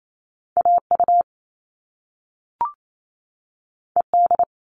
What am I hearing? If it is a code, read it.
Morse code, heard as a keyed tone, about 28 wpm, 700 Hz. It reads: AU ED
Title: All Cut Numbers - 2-Digit 28wpm